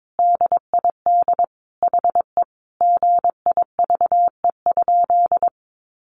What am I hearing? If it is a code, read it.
Morse code, heard as a keyed tone, about 22 wpm, 700 Hz. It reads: DID HE GIVE?
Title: Sentences from Top 1100 Words 22wpm